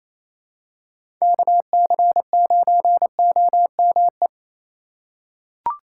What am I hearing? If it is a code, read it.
Morse code, heard as a keyed tone, about 28 wpm, 700 Hz. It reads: KC9OME
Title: USA Call Signs - 6 Characters - 4x Character Spacing - 28wpm